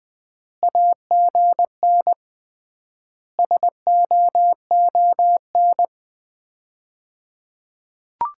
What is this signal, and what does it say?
Morse code, heard as a keyed tone, about 21 wpm, 700 Hz. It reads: AGN SOON
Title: Common QSO Phrases - 3x Word-Spacing - 20wpm